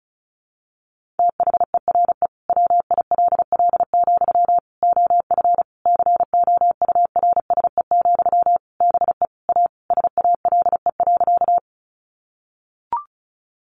Morse code, heard as a keyed tone, about 34 wpm, 700 Hz. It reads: THERE WILL, OF COURSE, BE A SALE.
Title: Sentences from Top 1400 Words 35wpm